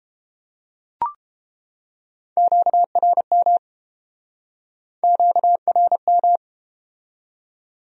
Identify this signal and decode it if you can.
Morse code, heard as a keyed tone, about 30 wpm, 700 Hz. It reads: QRM QRM